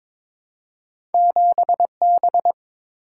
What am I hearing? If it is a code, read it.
Morse code, heard as a keyed tone, about 22 wpm, 700 Hz. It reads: 7B
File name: practice / 22wpm / Two Letter-Number Combinations - 22wpm